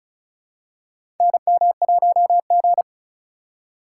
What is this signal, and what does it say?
Morse code, heard as a keyed tone, about 36 wpm, 700 Hz. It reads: NM1G
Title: USA Call Signs - 4 Characters - 3x Character Spacing - 35wpm